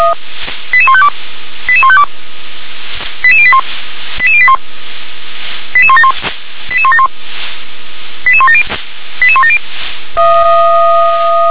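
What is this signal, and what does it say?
Five-tone selective call (ZVEI), heard as quick runs of five tones, tones of about 70 ms. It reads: 88152 88152 88081 88081 88171 88171 88170 88170
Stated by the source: ZVEI five-tone selective call received on an SDR.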